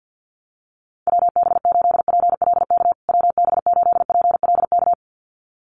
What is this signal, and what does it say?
Morse code, heard as a keyed tone, about 48 wpm, 700 Hz. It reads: WB8PLK WB8PLK
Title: USA Call Signs Repeated 2x 50wpm